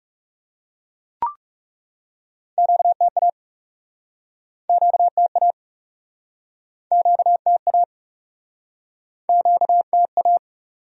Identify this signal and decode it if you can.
Morse code, heard as a keyed tone, about 41 wpm, 700 Hz. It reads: QTA QTA QTA QTA